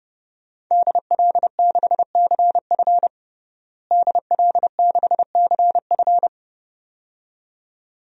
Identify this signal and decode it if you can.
Morse code, heard as a keyed tone, about 32 wpm, 700 Hz. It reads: DL6CF DL6CF
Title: DXCC-100 Call Signs - 5 Characters - Repeated 2x - 30wpm